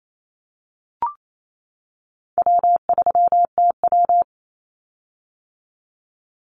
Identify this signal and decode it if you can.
Morse code, heard as a keyed tone, about 28 wpm, 700 Hz. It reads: W3TW
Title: Call Signs - 28wpm